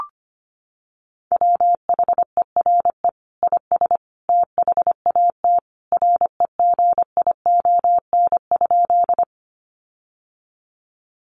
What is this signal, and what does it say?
Morse code, heard as a keyed tone, about 25 wpm, 700 Hz. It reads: WHERE IS THAT REGION?
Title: Sentences from Top 1500 Words 25wpm